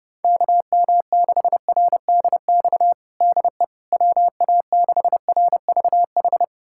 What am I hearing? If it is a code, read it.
Morse code, heard as a keyed tone, about 32 wpm, 700 Hz. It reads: KM6RDX DE WA6RVH
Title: USA Callsign DE Callsign - 6 Characters - 30wpm